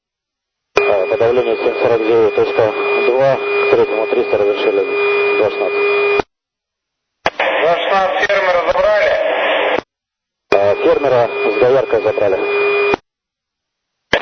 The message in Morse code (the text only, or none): none